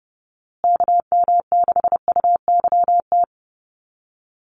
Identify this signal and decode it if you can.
Morse code, heard as a keyed tone, about 30 wpm, 700 Hz. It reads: KM6UYT
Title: USA Call Signs Repeated 2x 30wpm